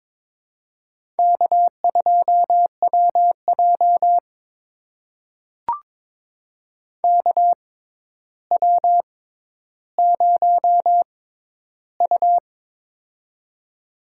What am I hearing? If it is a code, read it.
Morse code, heard as a keyed tone, about 22 wpm, spaced as for 11 wpm, 700 Hz. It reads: K2WJ KW0U
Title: USA Call Signs - 4 Characters - 6x Character Spacing - 22wpm